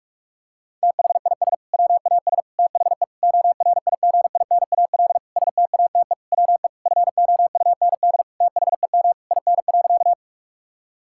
Morse code, heard as a keyed tone, about 46 wpm, 700 Hz. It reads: THIS WAS THE ORIGINAL STATE WE FOUND THEM IN.
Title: Sentences from Top 1000 Words - Review Part 6 - 45wpm